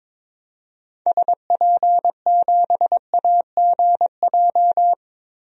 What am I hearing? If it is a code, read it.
Morse code, heard as a keyed tone, about 22 wpm, 700 Hz. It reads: SP7AGJ
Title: DXCC-100 Call Signs - 6 Characters - Repeated 2x - 22wpm